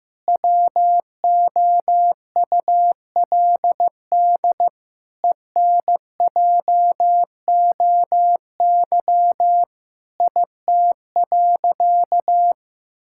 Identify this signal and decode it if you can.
Morse code, heard as a keyed tone, about 15 wpm, 700 Hz. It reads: WOULD ENJOY IT.
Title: Sentences from Top 1000 Words - Review Part 5 - 15wpm